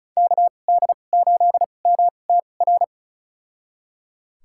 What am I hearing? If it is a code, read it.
Morse code, heard as a keyed tone, about 34 wpm, spaced as for 25 wpm, 700 Hz. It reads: KD8MTR
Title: USA Call Signs - 6 Characters - 2x Character Spacing - 35wpm